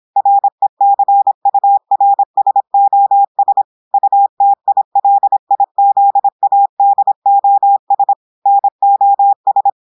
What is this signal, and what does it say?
Morse code, heard as a keyed tone, about 26 wpm, 800 Hz. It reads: RECURSOS UTILIZADOS NOS
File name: portugais texte en clair 2 vitesse 26 mots minutes